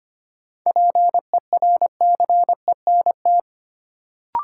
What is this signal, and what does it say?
Morse code, heard as a keyed tone, about 25 wpm, 700 Hz. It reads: PERCENT
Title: Sets of 2 Words - New Word Drill - 25wpm